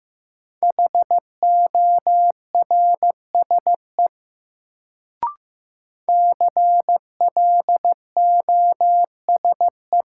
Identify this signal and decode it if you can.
Morse code, heard as a keyed tone, about 15 wpm, 700 Hz. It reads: HORSE CLOSE
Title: Sentences from Top 300 Words - New Word Drill - 15wpm